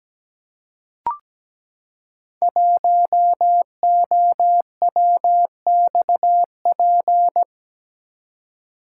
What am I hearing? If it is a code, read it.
Morse code, heard as a keyed tone, about 17 wpm, 700 Hz. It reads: 1OWXP
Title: Code Groups - 5 Letter:Numbers - 17wpm